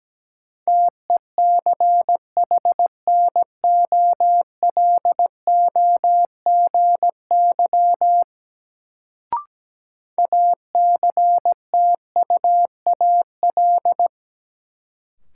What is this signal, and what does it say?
Morse code, heard as a keyed tone, about 17 wpm, 700 Hz. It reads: TECHNOLOGY ACTUAL
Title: Sentences from Top 1000 Words - Review Part 5 - New Word Drill - 17wpm